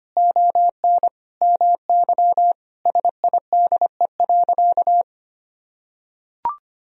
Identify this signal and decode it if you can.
Morse code, heard as a keyed tone, about 25 wpm, 700 Hz. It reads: ON MY SIDE.
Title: Sentences from Top 1000 Words - Review Part 1 - 25wpm